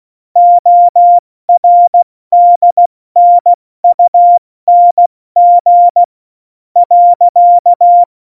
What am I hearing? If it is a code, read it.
Morse code, heard as a keyed tone, about 16 wpm, spaced as for 14 wpm, 700 Hz. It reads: ORDNUNG .
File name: Morsekurs / Witze / Witz 07 Juli 2020, 70 F80